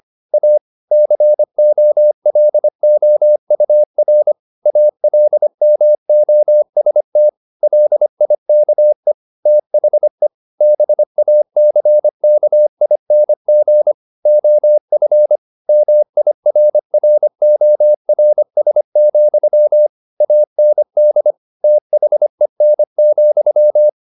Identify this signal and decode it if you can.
Morse code, heard as a keyed tone, about 25 wpm, 600 Hz. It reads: A COLOUR ALMOST LIKE THE BACKING OF MIRRORS, AND THEN,
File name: the old man and the sea part 1 25wpm